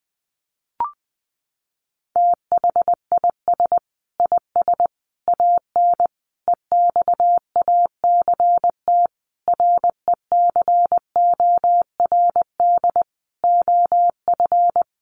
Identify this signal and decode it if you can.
Morse code, heard as a keyed tone, about 20 wpm, 700 Hz. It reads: THIS IS AN EXACT RECORD OF
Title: Sentences from Top 1300 Words 20wpm